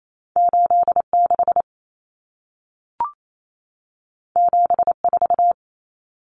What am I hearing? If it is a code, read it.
Morse code, heard as a keyed tone, about 28 wpm, 700 Hz. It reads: 86 74
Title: Numbers - 2-Digit 28wpm